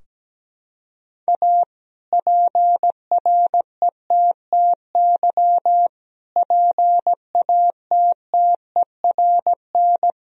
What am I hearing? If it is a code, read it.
Morse code, heard as a keyed tone, about 17 wpm, 700 Hz. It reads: A PRETTY PATTERN
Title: Sentences from Top 1000 Words - Review Part 2 - 17wpm